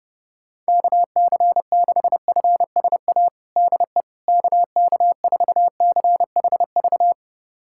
Morse code, heard as a keyed tone, about 30 wpm, 700 Hz. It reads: KC6FSA DE KK4CHV
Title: USA Callsign DE Callsign 30wpm